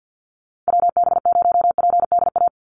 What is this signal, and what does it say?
Morse code, heard as a keyed tone, about 50 wpm, 700 Hz. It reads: WB0PDA
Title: USA Call Signs 50wpm